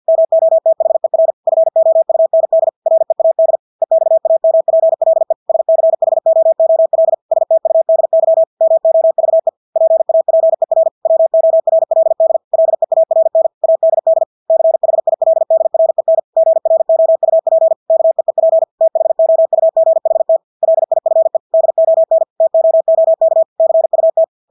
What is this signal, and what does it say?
Morse code, heard as a keyed tone, about 50 wpm, 650 Hz. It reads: MOTHER FOUND READ EXAMPLE SCHOOL STUDY MOVE PAPER WORLD LEARN ADD CHILDREN GROUP KEEP THOUGHT LIFE DON TOOK CUT
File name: Engelske ord 50 wpm